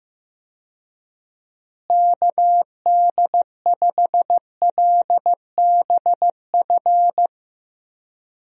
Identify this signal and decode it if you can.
Morse code, heard as a keyed tone, about 15 wpm, 700 Hz. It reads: KD5LBF
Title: USA Call Signs - 6 Characters - Repeated 3x - 15wpm